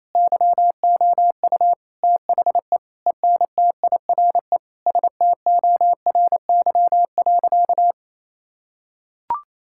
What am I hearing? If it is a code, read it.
Morse code, heard as a keyed tone, about 28 wpm, 700 Hz. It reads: YOU THE ENTIRE STORY.
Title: Sentences from Top 1000 Words - Review Part 5 - 28wpm